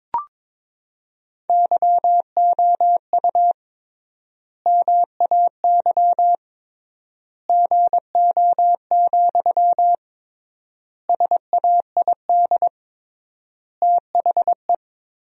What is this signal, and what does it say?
Morse code, heard as a keyed tone, about 22 wpm, 700 Hz. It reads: YOU MAY GO, SAID THE
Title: 3x Word-Spacing - Sentences from Top 400 Words - 22wpm